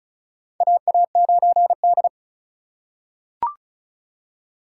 Morse code, heard as a keyed tone, about 36 wpm, 700 Hz. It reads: AA9D
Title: USA Call Signs - 4 Characters - 3x Character Spacing - 35wpm